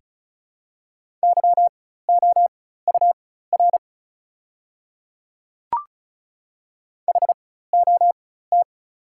Morse code, heard as a keyed tone, about 36 wpm, spaced as for 16 wpm, 690 Hz. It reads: YOUR HOT
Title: 4x Character Spacing - Top 100 Words - 35wpm